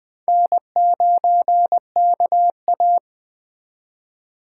Morse code, heard as a keyed tone, about 20 wpm, 700 Hz. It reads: N9KA
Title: USA Call Signs - 4 Characters - 3x Character Spacing - 20wpm